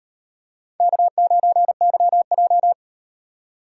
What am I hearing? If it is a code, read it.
Morse code, heard as a keyed tone, about 35 wpm, 700 Hz. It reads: K9YJ